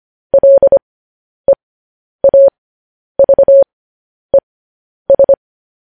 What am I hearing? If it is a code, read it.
Morse code, heard as a keyed tone, about 25 wpm, spaced as for 10 wpm, 550 Hz. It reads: LEAVES